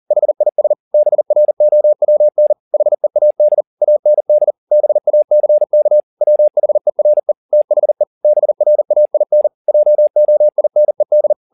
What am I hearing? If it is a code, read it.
Morse code, heard as a keyed tone, about 40 wpm, 600 Hz. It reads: HIS BROWN HEAD AND BACK WHERE THE BRAIN JOINED
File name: the old man and the sea 2 40wpm